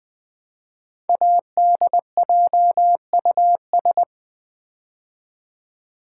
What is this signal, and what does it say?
Morse code, heard as a keyed tone, about 20 wpm, 700 Hz. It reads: ADJUS
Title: Build A Word - 5 Letters At A Time - 10-Letter Words - 20wpm